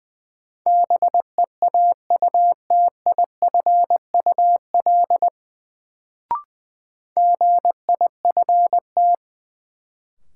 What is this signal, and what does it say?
Morse code, heard as a keyed tone, about 20 wpm, 700 Hz. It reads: BEAUTIFUL GIFT
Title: Sentences from Top 900 Words - New Word Drill - 20wpm